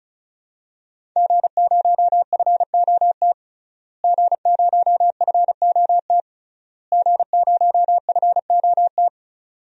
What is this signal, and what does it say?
Morse code, heard as a keyed tone, about 36 wpm, 700 Hz. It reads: G0FOT G0FOT G0FOT
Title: DXCC-100 Call Signs - 5 Characters - Repeated 3x - 35wpm